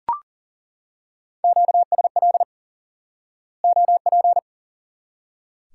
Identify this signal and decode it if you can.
Morse code, heard as a keyed tone, about 40 wpm, 700 Hz. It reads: QSL OP
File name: IQE Course 3 - Lesson 89 - YAGI - Two Elements - 40wpm